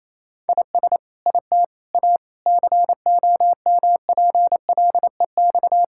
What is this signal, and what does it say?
Morse code, heard as a keyed tone, about 28 wpm, 700 Hz. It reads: IS IT A COMPLEX
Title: Sentences from Top 1000 Words - Review Part 6 - 28wpm